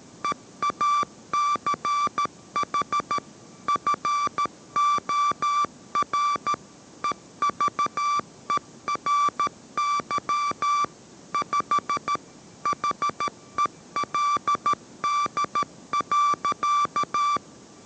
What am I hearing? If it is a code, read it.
Morse code, heard as a keyed tone, about 13 wpm, 1170 Hz. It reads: EACH FOR EVERY 5 HELD.